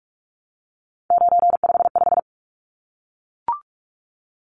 Morse code, heard as a keyed tone, about 44 wpm, 700 Hz. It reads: 955
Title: Numbers - 3-Digit 45wpm